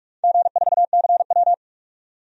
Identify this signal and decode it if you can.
Morse code, heard as a keyed tone, about 44 wpm, 700 Hz. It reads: G4CW
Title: DXCC-100 Call Signs - 4 Characters - 45wpm